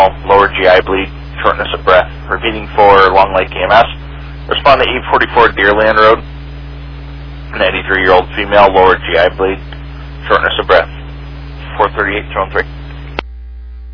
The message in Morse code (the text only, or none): none